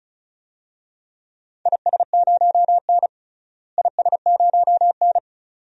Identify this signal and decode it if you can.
Morse code, heard as a keyed tone, about 34 wpm, 700 Hz. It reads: IS0N IS0N
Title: DXCC-100 Call Signs - 4 Characters - Repeated 2x - 35wpm